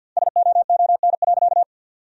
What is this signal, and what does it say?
Morse code, heard as a keyed tone, about 50 wpm, 700 Hz. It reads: SOON.
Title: 2x Word-Spacing - Sentences from Top 400 Words - 50wpm